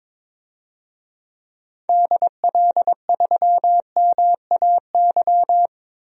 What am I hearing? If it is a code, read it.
Morse code, heard as a keyed tone, about 22 wpm, 700 Hz. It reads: DL3MAY
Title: DXCC-100 Call Signs - 6 Characters - Repeated 3x - 22wpm